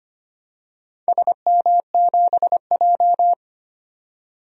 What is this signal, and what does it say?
Morse code, heard as a keyed tone, about 25 wpm, 700 Hz. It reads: SM7J
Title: DXCC-100 Call Signs - 4 Characters - 25wpm